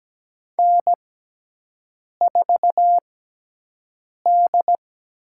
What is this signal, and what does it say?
Morse code, heard as a keyed tone, about 17 wpm, spaced as for 6 wpm, 700 Hz. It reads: N4D
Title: USA Call Signs - 5 Characters - 6x Character Spacing - 17wpm